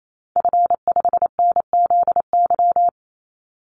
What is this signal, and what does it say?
Morse code, heard as a keyed tone, about 28 wpm, 700 Hz. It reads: F5NZY
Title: Call Signs - 28wpm